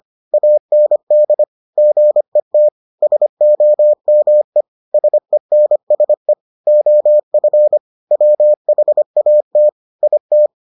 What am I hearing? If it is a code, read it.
Morse code, heard as a keyed tone, about 25 wpm, 600 Hz. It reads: AND GET SOME SENSE OF WHAT IT